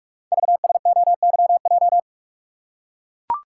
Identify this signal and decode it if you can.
Morse code, heard as a keyed tone, about 44 wpm, 700 Hz. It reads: VSOYJ
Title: Code Groups - 5 Letters - 45wpm